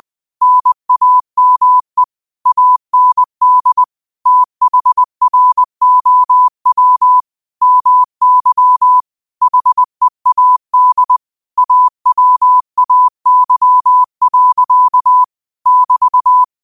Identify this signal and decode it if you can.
Morse code, heard as a keyed tone, about 20 wpm, 1000 Hz. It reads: NAME AND THROW MY HEAD AWAY. <BT>
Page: 20 WPM morse code quotes for Tue, 12 Nov 2024.